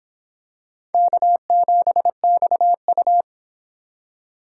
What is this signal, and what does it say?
Morse code, heard as a keyed tone, about 25 wpm, 700 Hz. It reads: K7XU